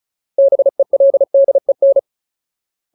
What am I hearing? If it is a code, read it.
Morse code, heard as a keyed tone, about 36 wpm, 550 Hz. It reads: BELDEN
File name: Ultimate Word List / 35WPM Ultimate